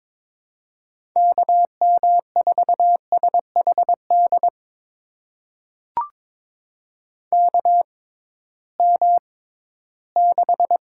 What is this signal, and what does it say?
Morse code, heard as a keyed tone, about 22 wpm, spaced as for 14 wpm, 700 Hz. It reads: KM4SHD KM6
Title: USA Call Signs - 6 Characters - 6x Character Spacing - 22wpm